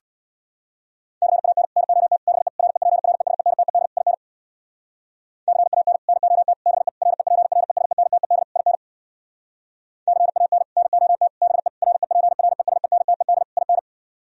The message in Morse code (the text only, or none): CAN NOT BE REPRESENTED IN CAN NOT BE REPRESENTED IN CAN NOT BE REPRESENTED IN